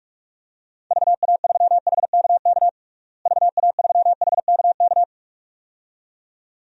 Morse code, heard as a keyed tone, about 41 wpm, 700 Hz. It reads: VA3HKK VA3HKK
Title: DXCC-100 Call Signs - 6 Characters - Repeated 2x - 45wpm